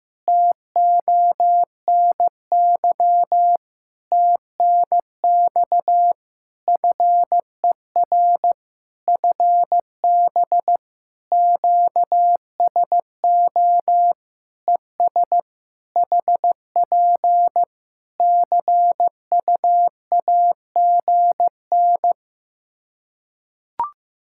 Morse code, heard as a keyed tone, about 15 wpm, 700 Hz. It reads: TONY TNX FER FB QSO ES HP CUAGN
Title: Practice QSO - Exchange 4 - Ending - 15wpm